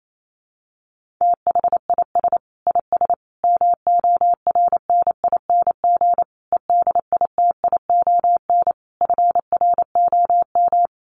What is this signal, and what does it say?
Morse code, heard as a keyed tone, about 28 wpm, 700 Hz. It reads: THIS IS MORNING EDITION FROM
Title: Sets of 5 Words Encore2 28wpm